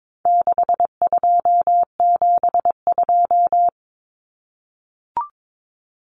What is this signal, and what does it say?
Morse code, heard as a keyed tone, about 22 wpm, 700 Hz. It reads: 6272
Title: Common Cut Numbers - 4-Digit 22wpm